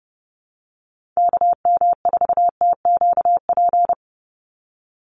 Morse code, heard as a keyed tone, about 30 wpm, 700 Hz. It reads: KM4TQP
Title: USA Call Signs 30wpm